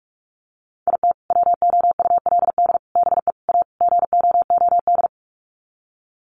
Morse code, heard as a keyed tone, about 44 wpm, 700 Hz. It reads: IT WOULD BE A GOOD
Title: Sets of 5 Words Encore2 45wpm